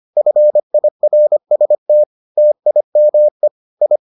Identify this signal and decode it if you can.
Morse code, heard as a keyed tone, about 25 wpm, 600 Hz. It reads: FIRST TIME I